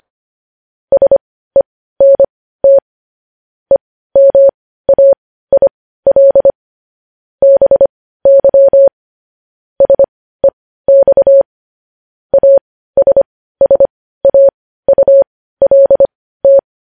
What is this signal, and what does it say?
Morse code, heard as a keyed tone, about 25 wpm, spaced as for 15 wpm, 550 Hz. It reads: SENT EMAIL BY SEX ASSAULT
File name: podcast / News headlines-1x-25x15